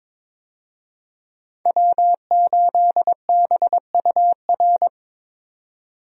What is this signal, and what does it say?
Morse code, heard as a keyed tone, about 22 wpm, 700 Hz. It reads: W8BUR
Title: USA Call Signs - 5 Characters - 22wpm